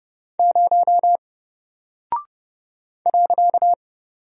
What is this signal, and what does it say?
Morse code, heard as a keyed tone, about 30 wpm, 700 Hz. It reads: 0 .
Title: Single Character (Letter:Number:Punctuation:Prosign) Alt 30wpm